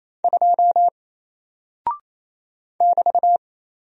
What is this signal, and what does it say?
Morse code, heard as a keyed tone, about 27 wpm, 700 Hz. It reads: 2 <BT>
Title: Single Character (Letter:Number:Punctuation:Prosign) Alt 28wpm